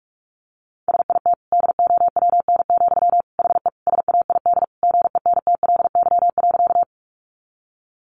Morse code, heard as a keyed tone, about 44 wpm, 700 Hz. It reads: SIT DOWN, HE SAID GENTLY.
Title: Sentences from Top 1500 Words 45wpm